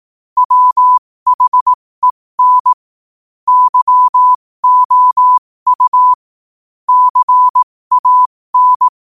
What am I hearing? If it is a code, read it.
Morse code, heard as a keyed tone, about 18 wpm, spaced as for 15 wpm, 1000 Hz. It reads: WHEN YOU CAN
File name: qotdfiles / 15wpm